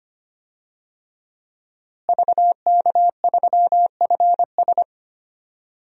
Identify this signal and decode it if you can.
Morse code, heard as a keyed tone, about 25 wpm, 700 Hz. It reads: VK3FS
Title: USA Call Signs - 5 Characters - Repeated 2x - 25wpm